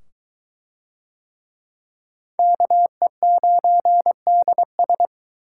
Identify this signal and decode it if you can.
Morse code, heard as a keyed tone, about 21 wpm, 700 Hz. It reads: KE9DS